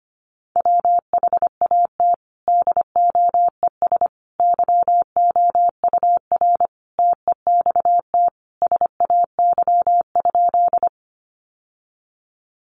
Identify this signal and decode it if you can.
Morse code, heard as a keyed tone, about 25 wpm, 700 Hz. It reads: WHAT DOES YOUR TEXT SAY?
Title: Sentences from Top 1200 Words 25wpm